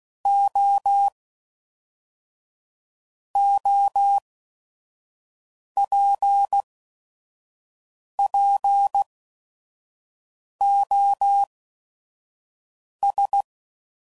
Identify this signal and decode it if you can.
Morse code, heard as a keyed tone, about 16 wpm, 800 Hz. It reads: O O P P O S